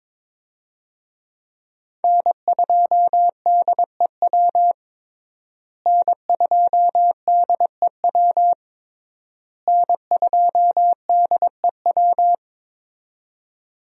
Morse code, heard as a keyed tone, about 21 wpm, 700 Hz. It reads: N2DEW N2DEW N2DEW
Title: USA Call Signs - 5 Characters - Repeated 3x - 22wpm